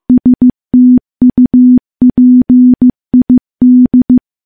Morse code, heard as a keyed tone, about 15 wpm, 250 Hz. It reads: STUPID
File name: morse / morse-training-15WPM